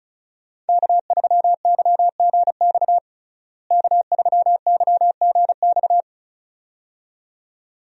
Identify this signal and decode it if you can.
Morse code, heard as a keyed tone, about 34 wpm, 700 Hz. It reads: K3YGX K3YGX
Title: USA Call Signs - 5 Characters - Repeated 2x - 35wpm